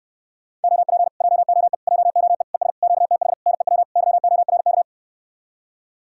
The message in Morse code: OP JOE JOE ES QTH NEW YORK

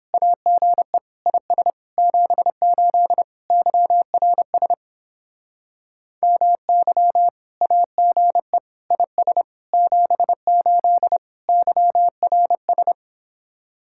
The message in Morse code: AGE IS 78 YRS MY AGE IS 78 YRS

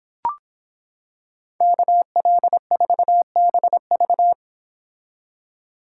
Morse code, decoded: KL4BV

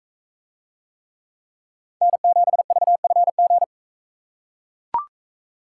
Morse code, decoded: N7VFG

